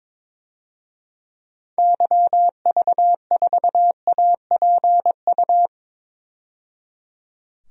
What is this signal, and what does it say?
Morse code, heard as a keyed tone, about 22 wpm, 700 Hz. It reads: YV4APU
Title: DXCC-100 Call Signs - 6 Characters - 22wpm